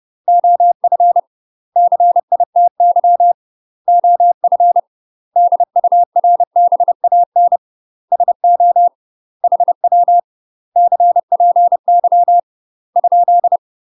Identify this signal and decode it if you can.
Morse code, heard as a keyed tone, about 30 wpm, 700 Hz. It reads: OF CITY OF DURBAN SO HW CPY ?
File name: UK MASTER QSOs TRAINING / 30WPM / UK MASTER QSOs 0024